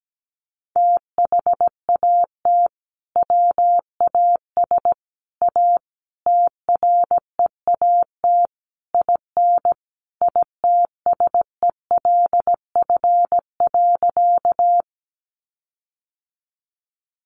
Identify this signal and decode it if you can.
Morse code, heard as a keyed tone, about 17 wpm, 700 Hz. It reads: THAT WAS A TREAT IN ITSELF.
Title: Sentences from Top 1300 Words 17wpm